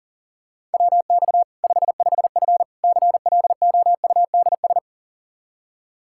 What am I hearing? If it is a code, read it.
Morse code, heard as a keyed tone, about 40 wpm, 700 Hz. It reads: WX 55F CLOUDS
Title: Practice QSO - Exchange 2 - Info - 40wpm